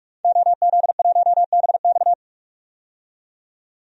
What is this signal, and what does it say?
Morse code, heard as a keyed tone, about 46 wpm, 700 Hz. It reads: OZ1BX
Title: DXCC-100 Call Signs - 5 Characters - 45wpm